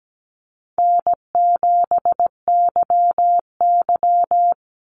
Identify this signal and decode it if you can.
Morse code, heard as a keyed tone, about 17 wpm, 700 Hz. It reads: N7YY
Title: NA SOTA Chaser Callsigns 17wpm